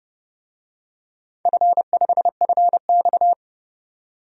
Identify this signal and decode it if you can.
Morse code, heard as a keyed tone, about 30 wpm, 700 Hz. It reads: F5FX
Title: DXCC-100 Call Signs - 4 Characters - 30wpm